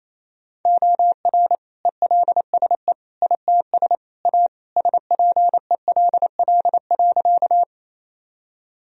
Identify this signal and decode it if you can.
Morse code, heard as a keyed tone, about 28 wpm, 700 Hz. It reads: OR ELSE ITS A SPELL.
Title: Sentences from Top 1000 Words - Review Part 1 - 28wpm